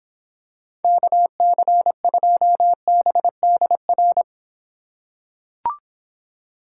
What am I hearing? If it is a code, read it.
Morse code, heard as a keyed tone, about 26 wpm, 700 Hz. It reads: KC2BDR